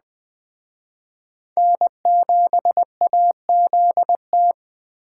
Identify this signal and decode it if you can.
Morse code, heard as a keyed tone, about 20 wpm, 700 Hz. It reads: N7AZT
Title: Speed Racing - USA 1x3 Call Signs 17wpm